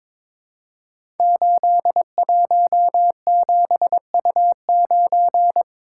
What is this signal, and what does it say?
Morse code, heard as a keyed tone, about 22 wpm, 700 Hz. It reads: 817U9
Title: Code Groups - 5 Letter:Numbers - 22wpm